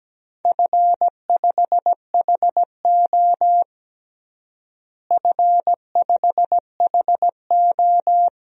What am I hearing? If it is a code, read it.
Morse code, heard as a keyed tone, about 17 wpm, 700 Hz. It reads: F5HO F5HO
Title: DXCC-100 Call Signs - 4 Characters - Repeated 3x - 17wpm